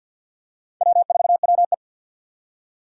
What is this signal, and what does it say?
Morse code, heard as a keyed tone, about 51 wpm, 700 Hz. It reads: W4WE
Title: Four Challenging Letter-Number Combinations 50wpm